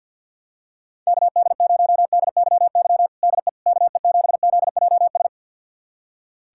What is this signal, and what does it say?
Morse code, heard as a keyed tone, about 47 wpm, 700 Hz. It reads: KD0DYY DE KE7ZJS
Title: USA Callsign DE Callsign - 6 Characters - 50wpm